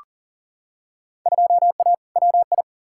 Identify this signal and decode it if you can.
Morse code, heard as a keyed tone, about 40 wpm, 700 Hz. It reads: 2A WI
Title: Field Day Exchanges - 40wpm